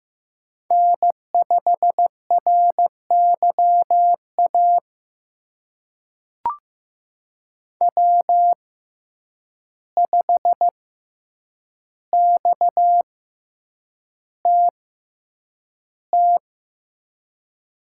Morse code, heard as a keyed tone, about 15 wpm, spaced as for 8 wpm, 700 Hz. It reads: N5RYA W5XTT
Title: USA Call Signs - 6x Character Spacing - 15wpm